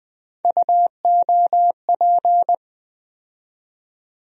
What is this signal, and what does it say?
Morse code, heard as a keyed tone, about 20 wpm, 700 Hz. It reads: UOP
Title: USA Call Sign Postfixes - 3 Letters 20wpm